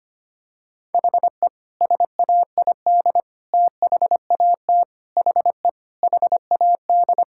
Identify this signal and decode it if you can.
Morse code, heard as a keyed tone, about 25 wpm, 700 Hz. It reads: HE SAID THAT HE HAD
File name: practice / 25wpm / Sentences with 3-Word Phrases - 25wpm